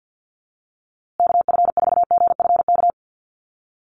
Morse code, heard as a keyed tone, about 50 wpm, 700 Hz. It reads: KF4GFK
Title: USA Call Signs Repeated 2x 50wpm